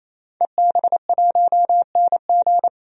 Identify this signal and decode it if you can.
Morse code, heard as a keyed tone, about 28 wpm, 700 Hz. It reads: EB1NG
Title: DXCC-100 Call Signs - 5 Characters - 28wpm